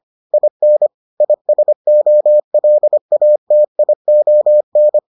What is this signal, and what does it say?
Morse code, heard as a keyed tone, about 25 wpm, 600 Hz. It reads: IN ISOLATION